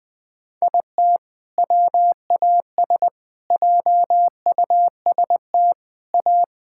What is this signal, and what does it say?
Morse code, heard as a keyed tone, about 21 wpm, 700 Hz. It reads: IT WAS JUST A
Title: Sentences with 3-Word Phrases - 20wpm